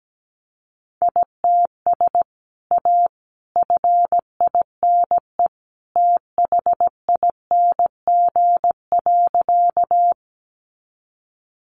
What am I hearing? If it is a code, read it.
Morse code, heard as a keyed tone, about 17 wpm, 700 Hz. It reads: ITS A FINE THING.
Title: Sentences from Top 600 Words 17wpm